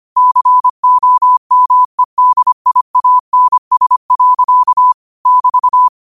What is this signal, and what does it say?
Morse code, heard as a keyed tone, about 25 wpm, 1000 Hz. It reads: COMEDIANS. <BT>
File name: qotdfiles / 25wpm